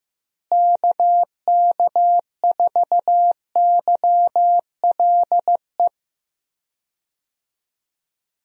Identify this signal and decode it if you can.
Morse code, heard as a keyed tone, about 15 wpm, 700 Hz. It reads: KK4YLE
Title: USA Call Signs - 6 Characters - Repeated 2x - 15wpm